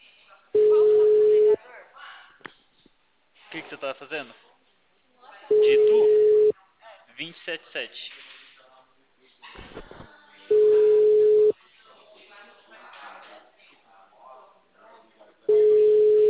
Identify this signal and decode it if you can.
Ringback tone (Europe, 425 Hz), with speech.